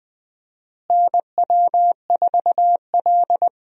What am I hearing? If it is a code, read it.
Morse code, heard as a keyed tone, about 20 wpm, 700 Hz. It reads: NW4L
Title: USA Call Signs - 4 Characters - Repeated 3x - 20wpm